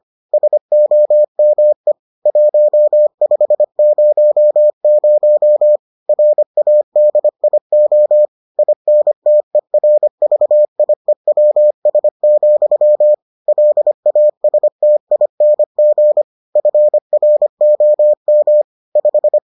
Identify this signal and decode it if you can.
Morse code, heard as a keyed tone, about 25 wpm, 600 Hz. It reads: SOME 1500 RADIO INTERVIEWS, LASTING FROM 5